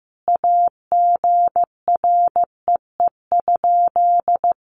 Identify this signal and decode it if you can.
Morse code, heard as a keyed tone, about 15 wpm, 700 Hz. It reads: AGREE?
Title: Sentences from Top 1100 Words 15wpm